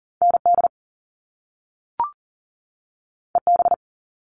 Morse code, heard as a keyed tone, about 40 wpm, 700 Hz. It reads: ND EB
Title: All Cut Numbers - 2-Digit 40wpm